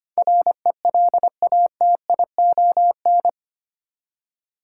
Morse code, heard as a keyed tone, about 25 wpm, 700 Hz. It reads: RELATION
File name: Eight Letter Words - 25wpm